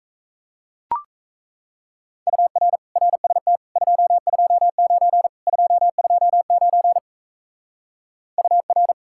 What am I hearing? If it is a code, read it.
Morse code, heard as a keyed tone, about 37 wpm, 700 Hz. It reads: UR RST 229 229 UR